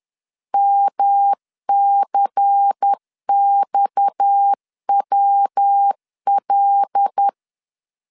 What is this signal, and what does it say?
Morse code, heard as a keyed tone, about 10 wpm, 800 Hz. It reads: MCXWL
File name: CW / DET / 1 9 1500C S A 10wpm 4511